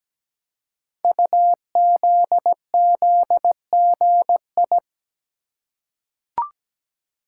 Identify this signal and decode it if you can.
Morse code, heard as a keyed tone, about 17 wpm, 700 Hz. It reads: UZZGI
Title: Code Groups - 5 Letters - 17wpm